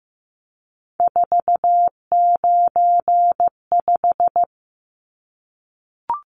Morse code, heard as a keyed tone, about 14 wpm, 700 Hz. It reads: 495
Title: Numbers - 3-Digit 15wpm